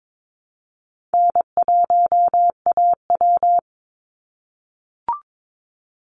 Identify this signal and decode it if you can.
Morse code, heard as a keyed tone, about 22 wpm, 700 Hz. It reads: N1AW
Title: NA SOTA Chaser Callsigns 22wpm